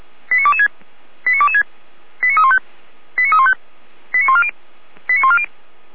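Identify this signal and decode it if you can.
Five-tone selective call (ZVEI), heard as quick runs of five tones, tones of about 70 ms. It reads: 89226 89226 89315 89315 89140 89140
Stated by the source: ZVEI five-tone selective call 89350 received by SDR.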